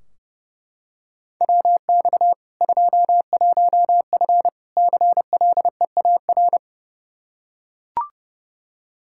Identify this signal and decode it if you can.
Morse code, heard as a keyed tone, about 30 wpm, 700 Hz. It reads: WX 21F CLEAR
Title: K1USN SST QSO Practice 30wpm